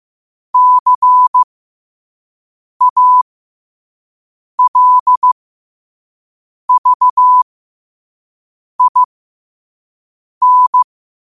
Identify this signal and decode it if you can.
Morse code, heard as a keyed tone, about 15 wpm, spaced as for 5 wpm, 1000 Hz. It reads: CALVIN
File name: qotdfiles / 5wpm